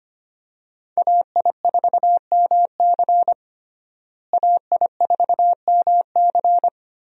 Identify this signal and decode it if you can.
Morse code, heard as a keyed tone, about 25 wpm, 700 Hz. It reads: AI4MC AI4MC
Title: USA Call Signs - 5 Characters - Repeated 3x - 25wpm